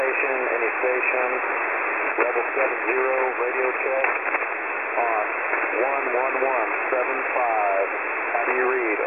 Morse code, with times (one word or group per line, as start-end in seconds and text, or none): none